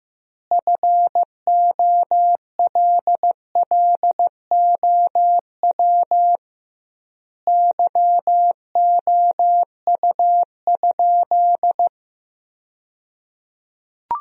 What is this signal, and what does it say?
Morse code, heard as a keyed tone, about 15 wpm, 700 Hz. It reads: FOLLOW YOU?
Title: 2x Word-Spacing - Sentences from Top 200 Words - 15wpm